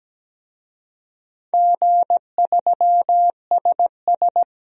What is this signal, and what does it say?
Morse code, heard as a keyed tone, about 17 wpm, 700 Hz. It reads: G3SS
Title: DXCC-100 Call Signs - 4 Characters - Repeated 3x - 17wpm